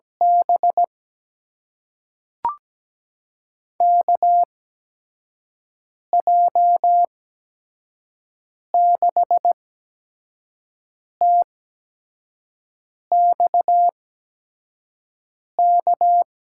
Morse code, heard as a keyed tone, about 17 wpm, spaced as for 5 wpm, 700 Hz. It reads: B KJ6TXK
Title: USA Call Signs - 6 Characters - 8x Character Spacing - 17wpm